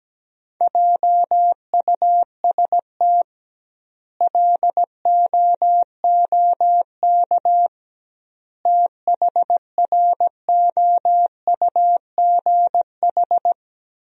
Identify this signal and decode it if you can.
Morse code, heard as a keyed tone, about 17 wpm, 700 Hz. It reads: JUST LOOK THROUGH
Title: 2x Word-Spacing - Sentences from Top 200 Words - 17wpm